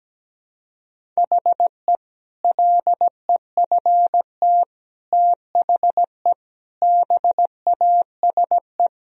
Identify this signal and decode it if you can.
Morse code, heard as a keyed tone, about 17 wpm, 700 Hz. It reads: HE LEFT THE BASE
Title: Sentences from Top 1000 Words - Review Part 1 - 17wpm